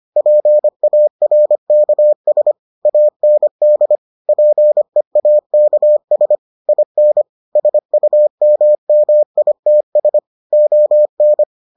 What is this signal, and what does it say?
Morse code, heard as a keyed tone, about 25 wpm, 600 Hz. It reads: PARKS AND PEAKS IN SUMMITS ON